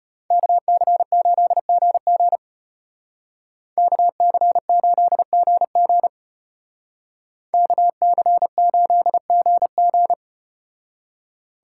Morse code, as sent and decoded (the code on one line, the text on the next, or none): -.- -.-. ---.. --. --. / -.- -.-. ---.. --. --. / -.- -.-. ---.. --. --.
KC8GG KC8GG KC8GG